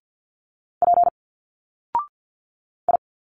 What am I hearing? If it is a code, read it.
Morse code, heard as a keyed tone, about 50 wpm, 700 Hz. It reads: P I
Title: Single Letters 50wpm